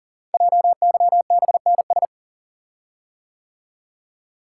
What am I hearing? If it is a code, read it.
Morse code, heard as a keyed tone, about 40 wpm, 700 Hz. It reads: JYBNS
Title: Code Groups - 5 Letters - 40wpm